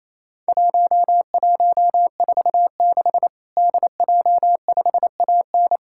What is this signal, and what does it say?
Morse code, heard as a keyed tone, about 28 wpm, 700 Hz. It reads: 1146 DJ5AN
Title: Worked All Europe DX Contest - QTC Exchange - 28wpm